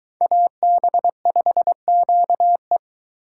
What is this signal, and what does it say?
Morse code, heard as a keyed tone, about 23 wpm, 700 Hz. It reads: AB5QE